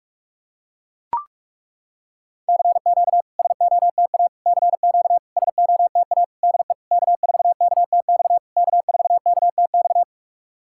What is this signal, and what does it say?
Morse code, heard as a keyed tone, about 41 wpm, 690 Hz. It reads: CQ SOTA CQ SOTA DE K4KTX K4KTX
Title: SOTA QSO Practice 45wpm